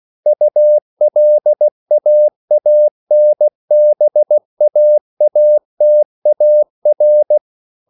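Morse code, heard as a keyed tone, about 16 wpm, 600 Hz. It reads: ULAANBAATAR
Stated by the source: CW practice of national capitals at 16 words per minute, from ON6ZQ.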